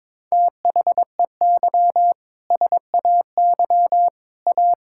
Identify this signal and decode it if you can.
Morse code, heard as a keyed tone, about 22 wpm, 700 Hz. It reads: THEY SAY A
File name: Sentences with 7-Word Phrases - 22wpm